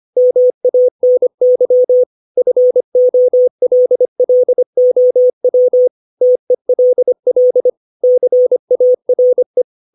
Morse code, heard as a keyed tone, about 25 wpm, 500 Hz. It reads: MANY FOLLOW TELL CARE
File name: 632 words / 632 Words 25WPM